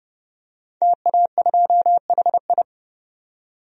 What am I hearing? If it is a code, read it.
Morse code, heard as a keyed tone, about 30 wpm, 700 Hz. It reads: TA2HI
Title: DXCC-100 Call Signs - 5 Characters - 30wpm